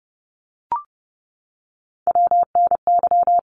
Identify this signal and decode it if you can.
Morse code, heard as a keyed tone, about 30 wpm, 700 Hz. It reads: WNY
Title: ARRL Sections 30wpm